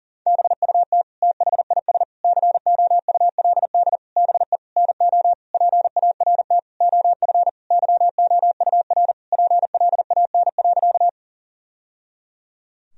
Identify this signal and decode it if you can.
Morse code, heard as a keyed tone, about 40 wpm, 700 Hz. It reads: BUT THIS COULD BE NO PART OF YOUR PLAN.
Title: Sentences from Top 1000 Words - Review Part 2 - 40wpm